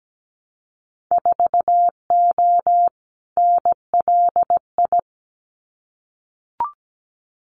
Morse code, heard as a keyed tone, about 17 wpm, 700 Hz. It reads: 4O NLI
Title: Winter Field Day Exchanges Encore 17wpm